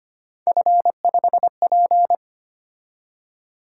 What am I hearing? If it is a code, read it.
Morse code, heard as a keyed tone, about 25 wpm, 700 Hz. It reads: F5P
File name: Three Letter-Number Combinations - 25wpm